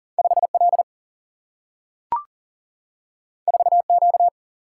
Morse code, heard as a keyed tone, about 42 wpm, 700 Hz. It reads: 5 L 4 Q
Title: Single-Letter Single-Number 40wpm